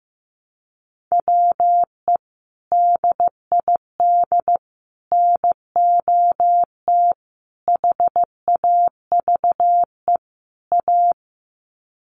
Morse code, heard as a keyed tone, about 15 wpm, 700 Hz. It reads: WE DID NOT HAVE A
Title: Sets of 5 Words Encore2 15wpm